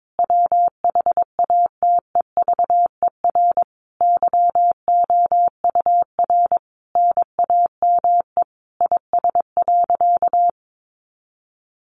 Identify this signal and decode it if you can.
Morse code, heard as a keyed tone, about 22 wpm, 700 Hz. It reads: WHATEVER YOUR NAME IS.
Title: Sentences from Top 1100 Words 22wpm